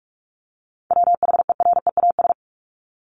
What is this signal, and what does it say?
Morse code, heard as a keyed tone, about 44 wpm, 700 Hz. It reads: WHEREAS
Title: Top 1501-1750 Words 45wpm